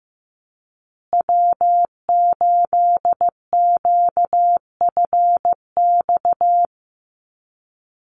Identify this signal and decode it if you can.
Morse code, heard as a keyed tone, about 15 wpm, 700 Hz. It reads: W8QFX
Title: USA Call Signs Repeated 2x 15wpm